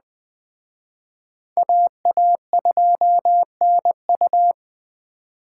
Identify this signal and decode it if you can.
Morse code, heard as a keyed tone, about 20 wpm, 700 Hz. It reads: AA2NU